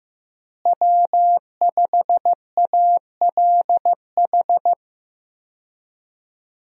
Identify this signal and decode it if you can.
Morse code, heard as a keyed tone, about 15 wpm, 700 Hz. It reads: W5ALH
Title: USA Call Signs - 5 Characters - Repeated 3x - 15wpm